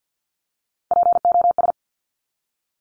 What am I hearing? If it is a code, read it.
Morse code, heard as a keyed tone, about 50 wpm, 700 Hz. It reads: POS
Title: Top Three Letter Combinations in Words 50wpm